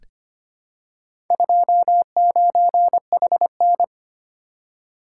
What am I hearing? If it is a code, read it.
Morse code, heard as a keyed tone, about 25 wpm, 700 Hz. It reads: 29HN
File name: practice / 25wpm / Four Letter-Number Combinations - 25wpm